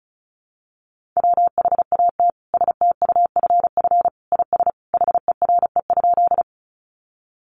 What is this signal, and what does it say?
Morse code, heard as a keyed tone, about 34 wpm, 700 Hz. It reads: WHAT STUFF IS HERE?
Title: Sentences from Top 1400 Words 35wpm